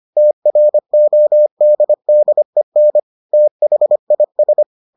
Morse code, heard as a keyed tone, about 25 wpm, 600 Hz. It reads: TRODDEN THIS